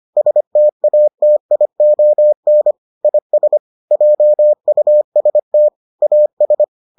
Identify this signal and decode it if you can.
Morse code, heard as a keyed tone, about 25 wpm, 600 Hz. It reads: STATION IS JUST AS